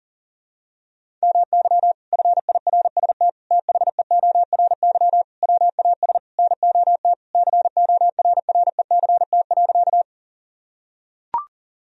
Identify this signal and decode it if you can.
Morse code, heard as a keyed tone, about 40 wpm, 700 Hz. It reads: MY FIRST THEORY WAS NOT CORRECT.
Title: Sentences from Top 1000 Words - Review Part 3 - 40wpm